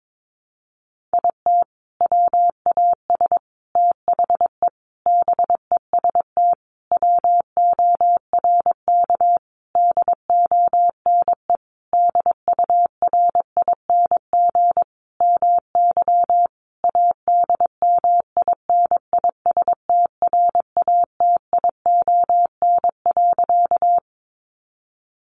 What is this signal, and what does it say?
Morse code, heard as a keyed tone, about 22 wpm, 700 Hz. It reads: IT WAS THE BEST WORK DONE DURING MY ADMINISTRATION.
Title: Sentences from Top 1300 Words 22wpm